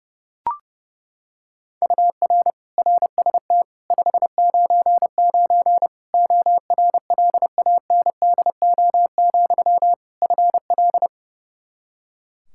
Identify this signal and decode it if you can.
Morse code, heard as a keyed tone, about 30 wpm, 700 Hz. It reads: UR RST 599 ORLANDO, FL
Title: Practice QSO - Exchange 1 - Beginning - 30wpm